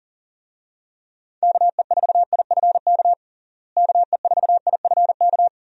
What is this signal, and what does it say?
Morse code, heard as a keyed tone, about 40 wpm, 700 Hz. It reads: KE4IFK KE4IFK
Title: USA Call Signs - 6 Characters - Repeated 3x - 40wpm